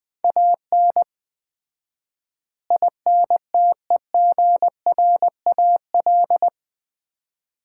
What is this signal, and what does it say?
Morse code, heard as a keyed tone, about 20 wpm, 700 Hz. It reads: AN INTEGRAL
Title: 4x Word-Spacing - Sets of 5 Words - 20wpm